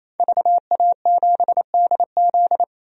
Morse code, heard as a keyed tone, about 28 wpm, 700 Hz. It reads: VA7DZ
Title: DXCC-100 Call Signs - 5 Characters - 28wpm